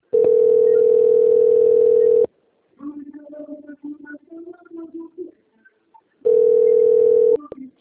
Ringback tone (North America), 440 + 480 Hz.